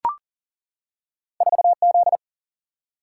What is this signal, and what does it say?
Morse code, heard as a keyed tone, about 40 wpm, 700 Hz. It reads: 4Z
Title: Single-Letter Single-Number 40wpm